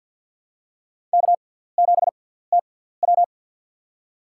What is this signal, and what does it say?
Morse code, heard as a keyed tone, about 47 wpm, spaced as for 17 wpm, 700 Hz. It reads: K7TW
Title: USA Call Signs - 4 Characters - 6x Character Spacing - 50wpm